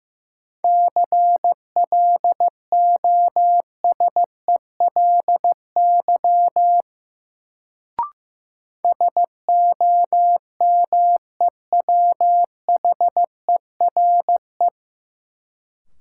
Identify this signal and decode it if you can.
Morse code, heard as a keyed tone, about 15 wpm, 700 Hz. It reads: CLOSELY SOMEWHERE
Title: Sentences from Top 1400 Words - New Word Drill - 15wpm